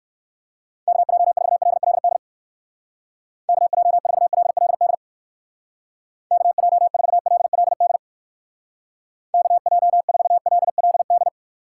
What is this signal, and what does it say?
Morse code, heard as a keyed tone, about 56 wpm, 700 Hz. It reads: KJ4LLD KJ4LLD KJ4LLD KJ4LLD
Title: Speed Racing - USA Call Signs 45wpm